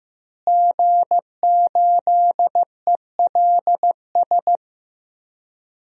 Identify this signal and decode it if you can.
Morse code, heard as a keyed tone, about 15 wpm, 700 Hz. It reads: G8ELS
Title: DXCC-100 Call Signs - 5 Characters - Repeated 2x - 15wpm